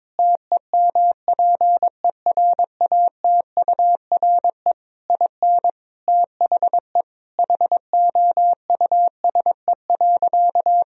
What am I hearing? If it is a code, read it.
Morse code, heard as a keyed tone, about 22 wpm, 700 Hz. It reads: TEMPERATURE IN THE HOUSE.
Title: Sentences from Top 1000 Words - Review Part 4 - 22wpm